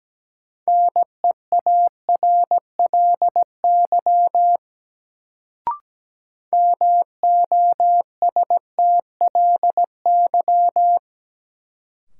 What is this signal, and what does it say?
Morse code, heard as a keyed tone, about 17 wpm, 700 Hz. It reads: NEARLY MOSTLY
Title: Sentences from Top 1300 Words - New Word Drill Alt - 17wpm